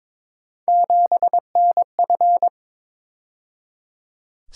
Morse code, heard as a keyed tone, about 26 wpm, 700 Hz. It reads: 7NF
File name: Speed-Racing - Three Letter-Number Combinations - 22wpm